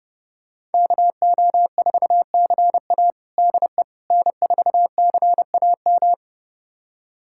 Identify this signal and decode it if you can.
Morse code, heard as a keyed tone, about 30 wpm, 700 Hz. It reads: KO4CA DE N4CAM
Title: USA Callsign DE Callsign - 5 Characters - 30wpm